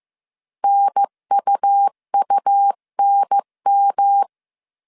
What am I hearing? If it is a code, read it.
Morse code, heard as a keyed tone, about 15 wpm, 800 Hz. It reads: NUUNM
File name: CW / DET / 1 3 1500C S A 14wpm 2447